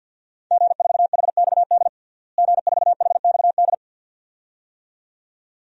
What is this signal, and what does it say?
Morse code, heard as a keyed tone, about 52 wpm, 700 Hz. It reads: G4HXD G4HXD
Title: DXCC-100 Call Signs - 5 Characters - Repeated 2x - 50wpm